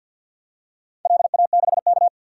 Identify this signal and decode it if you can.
Morse code, heard as a keyed tone, about 48 wpm, 700 Hz. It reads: LA6K
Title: DXCC-100 Call Signs - 4 Characters - 50wpm